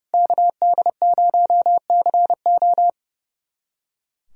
Morse code, heard as a keyed tone, about 30 wpm, 700 Hz. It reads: KD0CO
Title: USA Callsign DE Callsign - 5 Characters - 30wpm